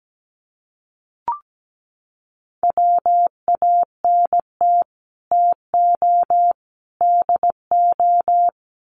Morse code, heard as a keyed tone, about 17 wpm, 700 Hz. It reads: WANT TO DO
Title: Sets of 3 Words 17wpm